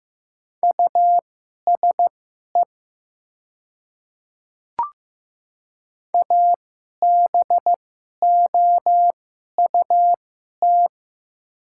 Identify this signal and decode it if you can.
Morse code, heard as a keyed tone, about 15 wpm, spaced as for 11 wpm, 700 Hz. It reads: USE ABOUT
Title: 2x Character Spacing - Top 100 Words - 15wpm